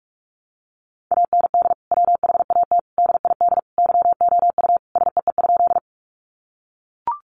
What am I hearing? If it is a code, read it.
Morse code, heard as a keyed tone, about 44 wpm, 700 Hz. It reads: AND WHAT DID YOU SEE?
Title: Sentences from Top 100 Words 45wpm